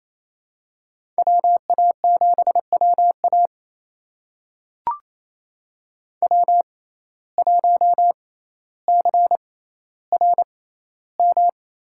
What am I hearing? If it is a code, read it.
Morse code, heard as a keyed tone, about 28 wpm, spaced as for 14 wpm, 700 Hz. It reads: WA7WA W1CRM
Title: USA Call Signs - 5 Characters - 6x Character Spacing - 28wpm